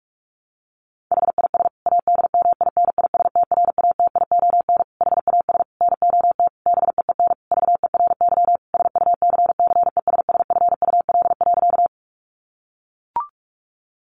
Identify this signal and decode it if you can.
Morse code, heard as a keyed tone, about 44 wpm, 700 Hz. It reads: HIS ADMINISTRATION HAS NOT BEEN VERY SUCCESSFUL.
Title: Sentences from Top 1300 Words 45wpm